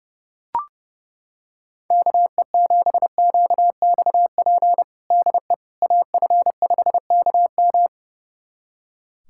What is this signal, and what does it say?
Morse code, heard as a keyed tone, about 30 wpm, 700 Hz. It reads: KE7QXP DE AF5KM
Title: USA Callsign DE Callsign 30wpm